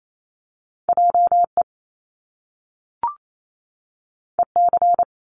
Morse code, heard as a keyed tone, about 28 wpm, 700 Hz. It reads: JE EC
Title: Top Two Letter Combinations in Words 28wpm